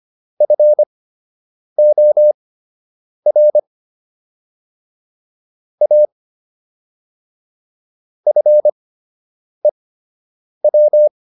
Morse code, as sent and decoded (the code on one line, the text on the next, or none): ..-. --- .-. / .- / ..-. .--
FOR A FEW